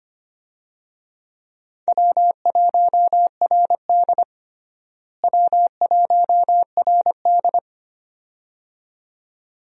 Morse code, heard as a keyed tone, about 25 wpm, 700 Hz. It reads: W1RD W1RD
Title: USA Call Signs - 4 Characters - Repeated 2x - 25wpm